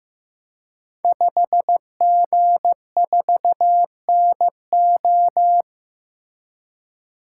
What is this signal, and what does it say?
Morse code, heard as a keyed tone, about 15 wpm, 700 Hz. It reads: 5G4NO
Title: Code Groups - 5 Letter:Numbers - 15wpm